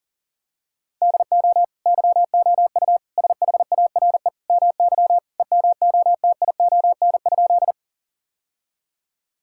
Morse code, heard as a keyed tone, about 40 wpm, 700 Hz. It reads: DO YOU SHARE MY EMOTION?
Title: Sentences from Top 1600 Words 40wpm